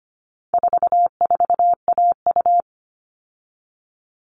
Morse code, heard as a keyed tone, about 25 wpm, 700 Hz. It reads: 44AU
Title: All Cut Numbers - 4-Digit 25wpm